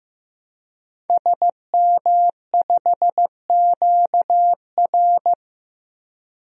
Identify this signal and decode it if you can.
Morse code, heard as a keyed tone, about 15 wpm, 700 Hz. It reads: SM5QR
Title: DXCC-100 Call Signs - 5 Characters - Repeated 3x - 15wpm